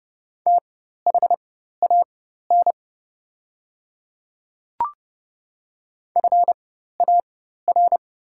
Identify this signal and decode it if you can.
Morse code, heard as a keyed tone, about 30 wpm, spaced as for 14 wpm, 700 Hz. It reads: THAN FAR
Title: 4x Character Spacing - Top 300 Words - 30wpm